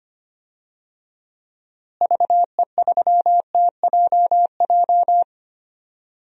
Morse code, heard as a keyed tone, about 25 wpm, 700 Hz. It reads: VE3TJJ
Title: DXCC-100 Call Signs - 6 Characters - Repeated 3x - 25wpm